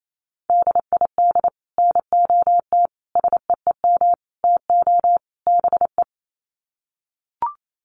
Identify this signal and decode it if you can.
Morse code, heard as a keyed tone, about 28 wpm, 700 Hz. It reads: DID NOT SEEM TO BE
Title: Sets of 5 Words Encore 28wpm